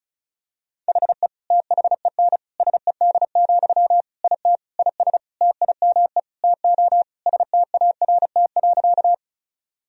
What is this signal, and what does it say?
Morse code, heard as a keyed tone, about 34 wpm, 700 Hz. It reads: HE THEN SED, IT IS TIME TO START.
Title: Sentences from Top 300 Words and Abbreviations 35wpm